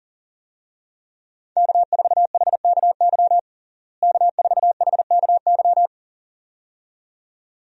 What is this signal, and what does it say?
Morse code, heard as a keyed tone, about 40 wpm, 700 Hz. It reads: K4HKY K4HKY
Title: USA Call Signs - 5 Characters - Repeated 2x - 40wpm